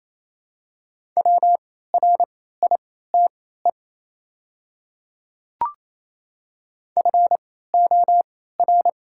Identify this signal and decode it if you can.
Morse code, heard as a keyed tone, about 27 wpm, spaced as for 16 wpm, 700 Hz. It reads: WRITE FOR
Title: 3x Character Spacing - Top 300 Words - 28wpm